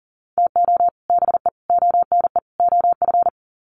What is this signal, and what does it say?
Morse code, heard as a keyed tone, about 40 wpm, 700 Hz. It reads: TO BE ONE OF
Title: Sets of 4 Words 40wpm